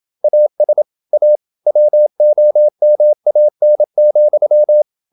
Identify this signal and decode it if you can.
Morse code, heard as a keyed tone, about 27 wpm, 600 Hz. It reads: AS A WOMAN,
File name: Never 0027wpm